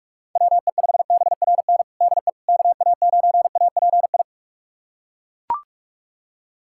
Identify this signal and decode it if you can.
Morse code, heard as a keyed tone, about 42 wpm, 700 Hz. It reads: WE5BRN DE KA9API
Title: USA Callsign DE Callsign - 6 Characters - 45wpm